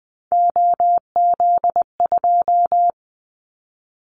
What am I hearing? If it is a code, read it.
Morse code, heard as a keyed tone, about 19 wpm, 700 Hz. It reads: OZ2
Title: Callsign Prefixes for Top 100 DXCC - 20wpm